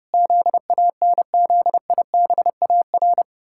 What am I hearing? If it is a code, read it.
Morse code, heard as a keyed tone, about 30 wpm, 700 Hz. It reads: ZANZIBAR
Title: Pangram Challenge v2 - 30wpm